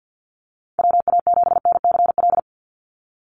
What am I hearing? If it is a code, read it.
Morse code, heard as a keyed tone, about 50 wpm, 700 Hz. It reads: WA7NCL
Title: NA SOTA Activator Callsigns 50wpm